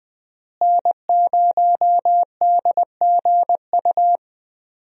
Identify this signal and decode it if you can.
Morse code, heard as a keyed tone, about 20 wpm, 700 Hz. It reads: N0DGU
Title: USA Call Signs - 5 Characters - Repeated 3x - 20wpm